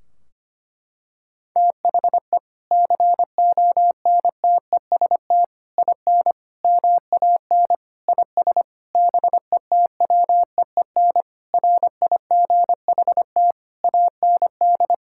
Morse code, heard as a keyed tone, about 25 wpm, 700 Hz. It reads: THE CONTEST IN MAN IS BETWEEN RIGHT AND
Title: Sentences from Top 1600 Words 25wpm